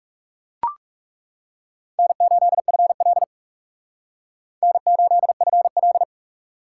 Morse code, heard as a keyed tone, about 36 wpm, 700 Hz. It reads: N8FL N8FL
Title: Speed Racing - USA Call Signs 30wpm